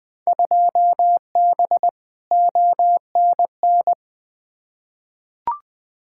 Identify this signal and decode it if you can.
Morse code, heard as a keyed tone, about 21 wpm, 700 Hz. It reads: 2B ONN
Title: Field Day Exchanges Encore - 20wpm